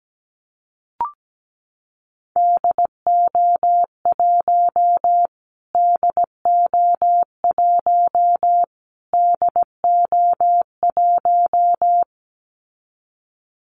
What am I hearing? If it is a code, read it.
Morse code, heard as a keyed tone, about 17 wpm, 700 Hz. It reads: DO1 DO1 DO1
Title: Callsign Prefixes for Top 25 DXCC - Repeated 3x - 17wpm